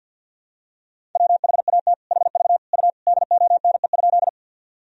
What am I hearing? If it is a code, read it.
Morse code, heard as a keyed tone, about 53 wpm, 700 Hz. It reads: WHAT HV U DONE?
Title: Sentences from Top 500 Words and Abbreviations 50wpm